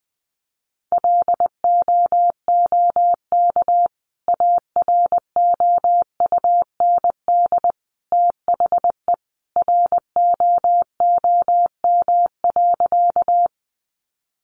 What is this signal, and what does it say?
Morse code, heard as a keyed tone, about 20 wpm, 700 Hz. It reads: LOOK AROUND THE ROOM.
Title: Sentences from Top 600 Words 20wpm